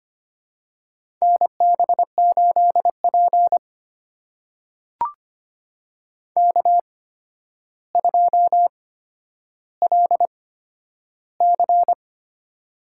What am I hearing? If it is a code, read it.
Morse code, heard as a keyed tone, about 25 wpm, spaced as for 11 wpm, 700 Hz. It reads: NB8P K2LC
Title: USA Call Signs - 4 Characters - 8x Character Spacing - 25wpm